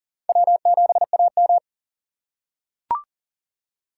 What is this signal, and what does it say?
Morse code, heard as a keyed tone, about 40 wpm, 700 Hz. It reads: W7AM
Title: USA Call Signs - 4 Characters - 6x Character Spacing - 40wpm